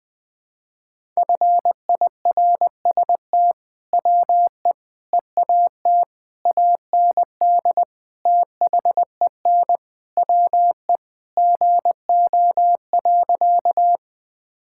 Morse code, heard as a keyed tone, about 20 wpm, 700 Hz. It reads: FIRST WE EAT AND THEN WE GO.
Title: Sentences with 2-Word Phrases - 20wpm